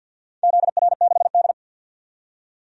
Morse code, heard as a keyed tone, about 51 wpm, 700 Hz. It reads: ZR6D
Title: DXCC-100 Call Signs - 4 Characters - 50wpm